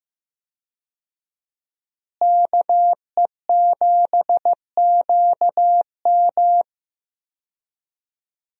Morse code, heard as a keyed tone, about 15 wpm, 700 Hz. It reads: KE7QM
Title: USA Call Signs - 5 Characters - Repeated 2x - 15wpm